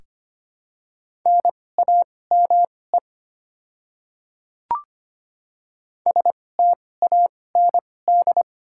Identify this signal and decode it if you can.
Morse code, heard as a keyed tone, about 25 wpm, spaced as for 18 wpm, 690 Hz. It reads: NAME STAND
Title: 2x Character Spacing - Top 300 Words - 25wpm